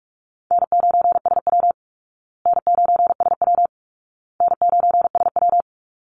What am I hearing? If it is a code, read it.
Morse code, heard as a keyed tone, about 44 wpm, 700 Hz. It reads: N9SW N9SW N9SW
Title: Easy Call Signs Repeated 3x - 45wpm